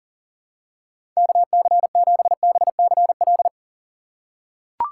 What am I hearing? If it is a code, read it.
Morse code, heard as a keyed tone, about 38 wpm, 700 Hz. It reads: KC7BCL
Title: USA Call Signs - 6x Character Spacing - 40wpm